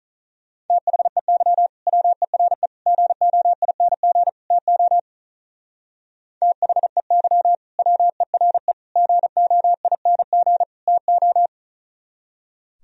Morse code, heard as a keyed tone, about 42 wpm, 700 Hz. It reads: THEY WERE GOING TO THEY WERE GOING TO
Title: Speed-Racing - Sets of 4 Words 35wpm